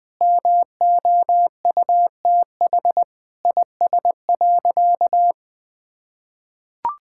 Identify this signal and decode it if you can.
Morse code, heard as a keyed tone, about 21 wpm, 700 Hz. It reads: MOUTH IS.
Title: Sentences with 7-Word Phrases - 20wpm